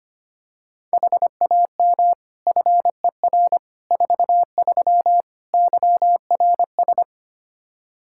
Morse code, heard as a keyed tone, about 25 wpm, 700 Hz. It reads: HAM FER 43 YRS
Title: K1USN SST QSO Practice 25wpm